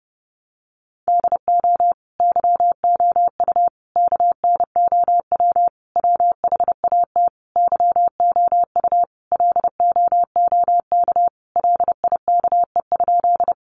Morse code, heard as a keyed tone, about 30 wpm, 700 Hz. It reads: DO YOU KNOW WHAT YOU LOOK LIKE?
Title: Sentences from Top 100 Words 30wpm